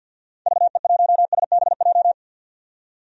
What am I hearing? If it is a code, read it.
Morse code, heard as a keyed tone, about 46 wpm, 700 Hz. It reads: VE1SBJ
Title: DXCC-100 Call Signs - 6 Characters - 50wpm